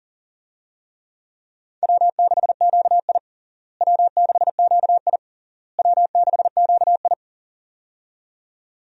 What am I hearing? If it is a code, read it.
Morse code, heard as a keyed tone, about 37 wpm, 700 Hz. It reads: W6QI W6QI W6QI
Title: USA Call Signs - 4 Characters - Repeated 3x - 40wpm